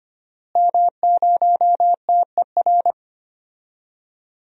Morse code, heard as a keyed tone, about 25 wpm, 700 Hz. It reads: M0TER
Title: DXCC-100 Call Signs - 5 Characters - Repeated 2x - 25wpm